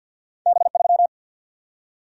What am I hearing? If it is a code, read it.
Morse code, heard as a keyed tone, about 54 wpm, 700 Hz. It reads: B3
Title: Single-Letter Single-Number 50wpm